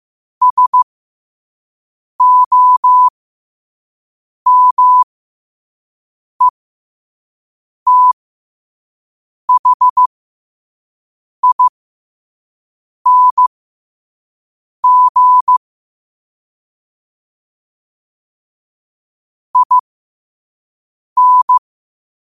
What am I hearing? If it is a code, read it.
Morse code, heard as a keyed tone, about 15 wpm, spaced as for 5 wpm, 1000 Hz. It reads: SOMETHING IN
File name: qotdfiles / 5wpm